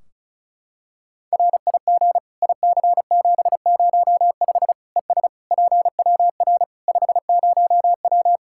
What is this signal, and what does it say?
Morse code, heard as a keyed tone, about 36 wpm, 700 Hz. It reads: RIG IC705 ES PWR 50W
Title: Practice QSO - Exchange 2 - Info - 35wpm